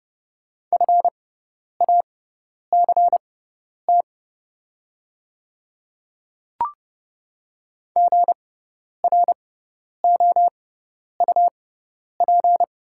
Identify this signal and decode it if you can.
Morse code, heard as a keyed tone, about 30 wpm, spaced as for 10 wpm, 690 Hz. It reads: FACT GROUP
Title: 6x Character Spacing - Top 500 Words - 30wpm